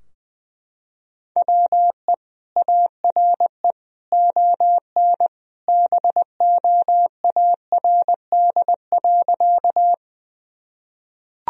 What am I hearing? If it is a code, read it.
Morse code, heard as a keyed tone, about 20 wpm, 700 Hz. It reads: WE ARE ON BOARD.
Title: Sentences from Top 1000 Words - Review Part 4 - 20wpm